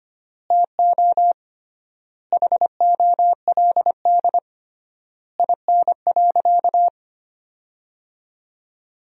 Morse code, heard as a keyed tone, about 25 wpm, 700 Hz. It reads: TO HOLD IN.
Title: 3x Word-Spacing - Sentences from Top 400 Words - 25wpm